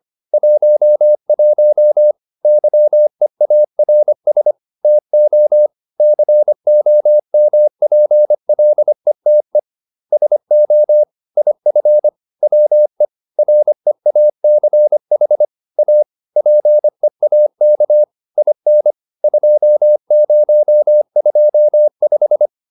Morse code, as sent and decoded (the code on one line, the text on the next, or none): .---- .---- / -.-- . .- .-. ... / - --- / -.-. --- -- .--. .-.. - . / ... --- / .. ..-. / .-- . / .-. .- -.-. .... / .- / .--. .- -.- / .. -. / ..--- ----- ..--- .....
11 YEARS TO COMPLETE SO IF WE REACH A PEAK IN 2025